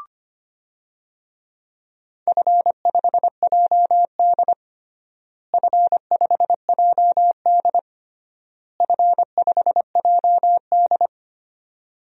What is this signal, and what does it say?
Morse code, heard as a keyed tone, about 25 wpm, 700 Hz. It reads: F5JD F5JD F5JD
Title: DXCC-100 Call Signs - 4 Characters - Repeated 3x - 25wpm